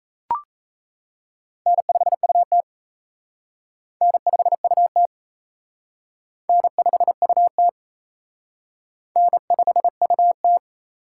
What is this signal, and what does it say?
Morse code, heard as a keyed tone, about 33 wpm, 700 Hz. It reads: N5UT N5UT N5UT N5UT
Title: Speed Racing - USA 1x2 Call Signs 28wpm